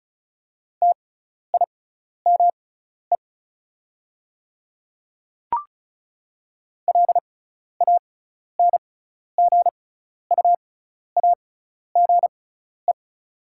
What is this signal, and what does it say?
Morse code, heard as a keyed tone, about 34 wpm, spaced as for 12 wpm, 690 Hz. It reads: TIME LANGUAGE
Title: 6x Character Spacing - Top 500 Words - 35wpm